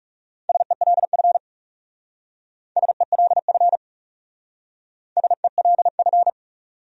Morse code, heard as a keyed tone, about 38 wpm, 700 Hz. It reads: SELF SELF SELF